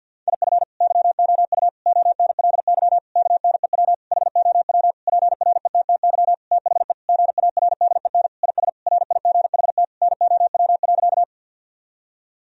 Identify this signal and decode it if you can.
Morse code, heard as a keyed tone, about 51 wpm, 700 Hz. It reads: IF YOU ONLY KNEW HOW PRETTY THE GARDEN IS RIGHT NOW.
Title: Sentences from Top 1000 Words - Review Part 4 - 50wpm